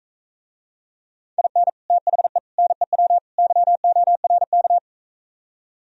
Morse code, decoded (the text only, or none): IN THE NEW YORK